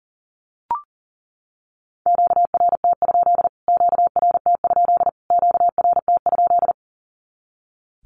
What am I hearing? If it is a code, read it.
Morse code, heard as a keyed tone, about 38 wpm, 700 Hz. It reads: QRT? QRT? QRT?
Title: Q-Codes Repeated 3x 40wpm